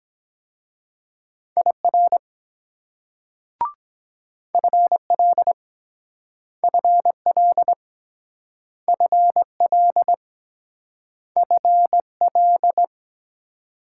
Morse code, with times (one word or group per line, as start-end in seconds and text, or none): 1.6-2.2 IR
4.5-5.5 FL
6.6-7.7 FL
8.9-10.1 FL
11.4-12.8 FL